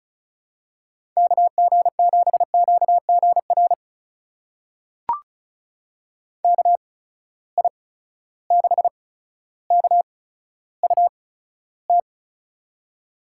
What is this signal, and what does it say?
Morse code, heard as a keyed tone, about 36 wpm, spaced as for 15 wpm, 700 Hz. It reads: KG7QGR KI6KUT
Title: USA Call Signs - 6 Characters - 8x Character Spacing - 35wpm